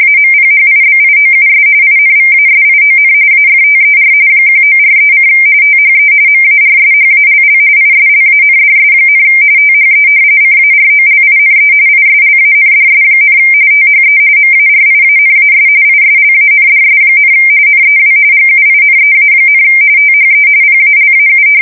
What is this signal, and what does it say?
RTTY, about 45.45 baud. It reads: AMPHIBIOUS INVASION AFTER 1943: THE MARSHALL ISLANDS, THE CAROLINE ISLANDS, THE MARIANAS ISLANDS, NEW GUINEA, THE PALAU
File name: fsk170 / battleshipnj